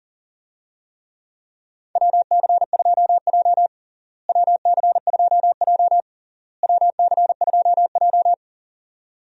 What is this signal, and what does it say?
Morse code, heard as a keyed tone, about 40 wpm, 700 Hz. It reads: WC2J WC2J WC2J
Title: USA Call Signs - 4 Characters - Repeated 3x - 40wpm